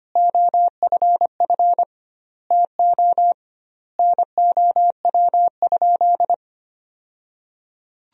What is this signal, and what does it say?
Morse code, heard as a keyed tone, about 25 wpm, 700 Hz. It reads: OFF TO NOW?
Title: 2x Word-Spacing - Sentences from Top 200 Words - 25wpm